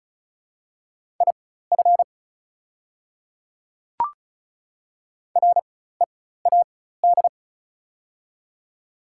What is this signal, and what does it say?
Morse code, heard as a keyed tone, about 34 wpm, spaced as for 16 wpm, 700 Hz. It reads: IF READ
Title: 4x Character Spacing - Top 300 Words - 35wpm